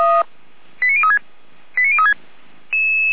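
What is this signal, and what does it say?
Five-tone selective call (ZVEI), heard as quick runs of five tones, tones of about 70 ms. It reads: 89036 89036
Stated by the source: ZVEI five-tone call 89187 received on an SDR.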